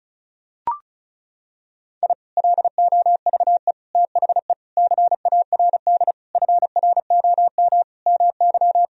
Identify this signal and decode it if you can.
Morse code, heard as a keyed tone, about 34 wpm, 700 Hz. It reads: I LOVE THE CARD FROM MY
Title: Sentences from Top 1000 Words - Review Part 4 - 35wpm